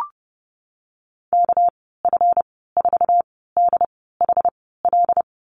Morse code, heard as a keyed tone, about 30 wpm, spaced as for 17 wpm, 700 Hz. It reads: KF4DHL
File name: Lesson 37g - Letter x - Call Signs 30wpm